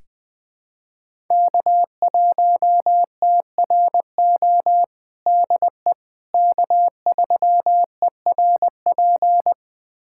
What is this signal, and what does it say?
Morse code, heard as a keyed tone, about 20 wpm, 700 Hz. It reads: K1TRO DE K3ERP
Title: USA Callsign DE Callsign 20wpm